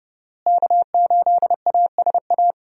Morse code, heard as a keyed tone, about 30 wpm, 700 Hz. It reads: K8ASA
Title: USA Call Signs - 5 Characters - Repeated 3x - 30wpm